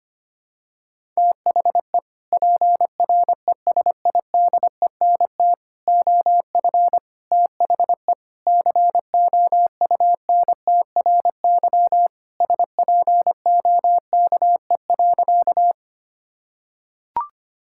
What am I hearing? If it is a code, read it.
Morse code, heard as a keyed tone, about 25 wpm, 700 Hz. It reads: THE PRESIDENT OF THE COUNTRY SPOKE.
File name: practice / 25wpm / Sentences with 3-Word Phrases - 25wpm